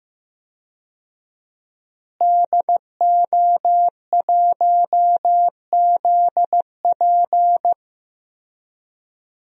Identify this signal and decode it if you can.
Morse code, heard as a keyed tone, about 15 wpm, 700 Hz. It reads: DO1ZP
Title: DXCC-100 Call Signs - 5 Characters - Repeated 3x - 15wpm